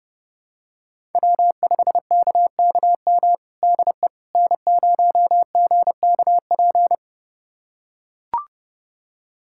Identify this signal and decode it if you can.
Morse code, heard as a keyed tone, about 30 wpm, 700 Hz. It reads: W5KKM DE N0GKP
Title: USA Callsign DE Callsign - 5 Characters - 30wpm